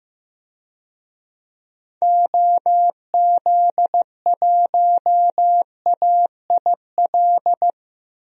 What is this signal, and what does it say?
Morse code, heard as a keyed tone, about 15 wpm, 700 Hz. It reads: OZ1AIL
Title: DXCC-100 Call Signs - 6 Characters - 15wpm